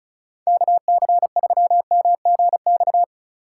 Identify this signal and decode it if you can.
Morse code, heard as a keyed tone, about 34 wpm, 690 Hz. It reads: KC3MGX
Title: USA Call Signs - 6 Characters - 6x Character Spacing - 35wpm